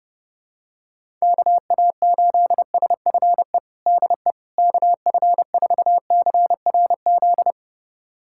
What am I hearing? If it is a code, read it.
Morse code, heard as a keyed tone, about 32 wpm, 700 Hz. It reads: KA8SFE DE KF4CRZ
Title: USA Callsign DE Callsign - 6 Characters - 30wpm